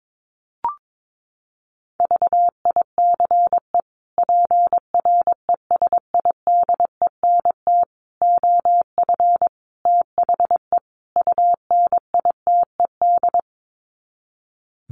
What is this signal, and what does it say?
Morse code, heard as a keyed tone, about 22 wpm, 700 Hz. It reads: VICE PRESIDENT OF THE UNITED
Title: Sets of 5 Words Encore2 22wpm